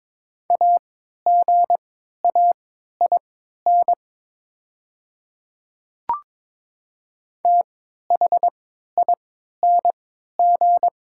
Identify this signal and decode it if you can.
Morse code, heard as a keyed tone, about 22 wpm, spaced as for 12 wpm, 700 Hz. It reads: AGAIN THING
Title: 3x Character Spacing - Top 200 Words - 22wpm